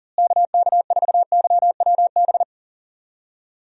Morse code, heard as a keyed tone, about 40 wpm, 700 Hz. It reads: KK4YWB
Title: USA Call Signs - 6 Characters - Repeated 3x - 40wpm